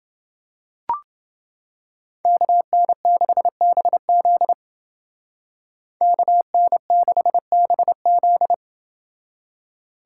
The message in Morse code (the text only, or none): KN6BZ KN6BZ